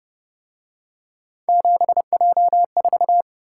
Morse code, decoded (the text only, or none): 7J4